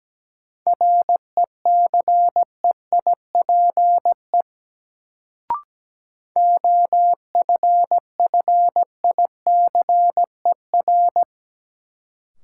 Morse code, heard as a keyed tone, about 17 wpm, 700 Hz. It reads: RECEIPE OFFICER
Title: Sentences from Top 1200 Words - New Word Drill - 17wpm